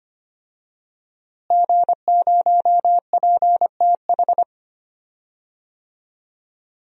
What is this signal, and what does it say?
Morse code, heard as a keyed tone, about 25 wpm, 700 Hz. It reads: G0PTH
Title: DXCC-100 Call Signs - 5 Characters - Repeated 2x - 25wpm